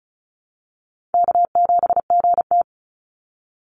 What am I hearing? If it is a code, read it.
Morse code, heard as a keyed tone, about 36 wpm, 700 Hz. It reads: K7GT
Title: NA SOTA Chaser Callsigns 35wpm